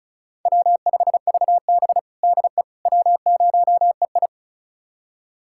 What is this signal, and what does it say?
Morse code, heard as a keyed tone, about 34 wpm, 700 Hz. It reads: W5VB DE W0EI
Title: USA Callsign DE Callsign - 4 Characters - 35wpm